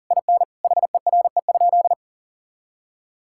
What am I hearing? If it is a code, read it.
Morse code, heard as a keyed tone, about 40 wpm, 700 Hz. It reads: IN HERE?
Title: Sentences from Top 1000 Words - Review Part 6 - 40wpm